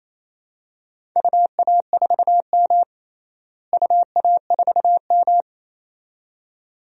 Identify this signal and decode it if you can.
Morse code, heard as a keyed tone, about 28 wpm, 700 Hz. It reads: UA4M UA4M
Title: DXCC-100 Call Signs - 4 Characters - Repeated 2x - 28wpm